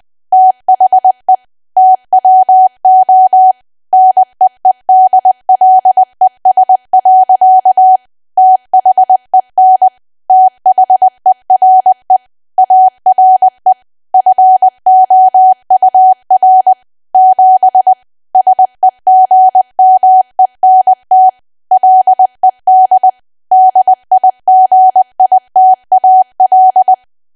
Morse code, heard as a keyed tone, about 20 wpm, 750 Hz. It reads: THE TWO NEEDLES. THEN THERE ARE FOUR 7 SEGMENT LED DIGITAL
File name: cwa / W1AW-Level3 / 20WPM